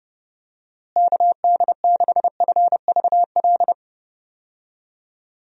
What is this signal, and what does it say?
Morse code, heard as a keyed tone, about 30 wpm, 700 Hz. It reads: KD6FVL
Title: USA Call Signs - 6 Characters - Repeated 2x - 30wpm